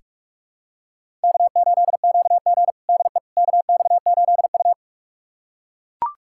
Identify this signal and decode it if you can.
Morse code, heard as a keyed tone, about 42 wpm, 700 Hz. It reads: K8QG DE KX8U
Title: USA Callsign DE Callsign - 4 Characters - 45wpm